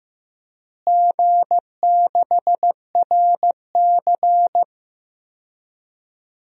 Morse code, heard as a keyed tone, about 15 wpm, 700 Hz. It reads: G6RC
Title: DXCC-100 Call Signs - 4 Characters - Repeated 2x - 15wpm